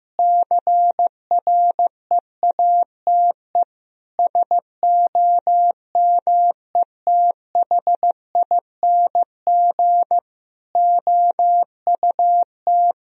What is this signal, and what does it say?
Morse code, heard as a keyed tone, about 15 wpm, 700 Hz. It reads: CREATE SOMETHING OUT
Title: Sentences from Top 1000 Words - Review Part 3 - 15wpm